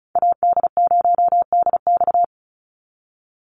Morse code, heard as a keyed tone, about 34 wpm, 700 Hz. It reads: AD0DX
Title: Call Signs - 35wpm